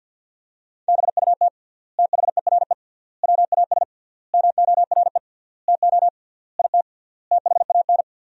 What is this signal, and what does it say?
Morse code, heard as a keyed tone, about 50 wpm, 700 Hz. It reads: BUT THERE WAS MORE TO IT THAN
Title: 3x Word-Spacing - Sentences from Top 100 Words - 50wpm